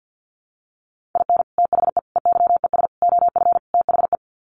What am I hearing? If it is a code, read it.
Morse code, heard as a keyed tone, about 50 wpm, 700 Hz. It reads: IN THE EYES OF THE
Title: Sets of 5 Words 50wpm